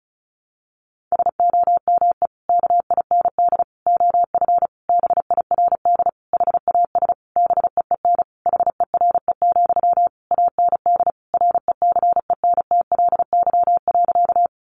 Morse code, heard as a keyed tone, about 34 wpm, 700 Hz. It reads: SOME KIND OF BIRD HAS BEEN HERE, AND RECENTLY.
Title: Sentences from Top 1200 Words 35wpm